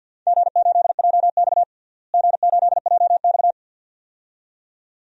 Morse code, heard as a keyed tone, about 50 wpm, 700 Hz. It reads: G8JX G8JX
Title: DXCC-100 Call Signs - 4 Characters - Repeated 2x - 50wpm